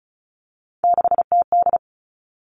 Morse code, heard as a keyed tone, about 36 wpm, 700 Hz. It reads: 6TD
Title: Three Challenging Letter-Number Combinations 35wpm